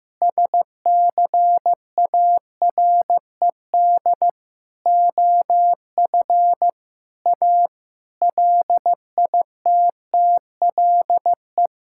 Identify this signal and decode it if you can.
Morse code, heard as a keyed tone, about 15 wpm, 700 Hz. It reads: SCARED OF A LITTLE
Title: Sentences from Top 1000 Words - Review Part 5 - 15wpm